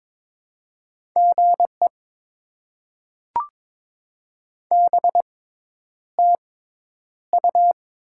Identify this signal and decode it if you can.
Morse code, heard as a keyed tone, about 23 wpm, spaced as for 10 wpm, 690 Hz. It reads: GE BTU
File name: Top 20 QSO Elements - 6x Character Spacing - 22wpm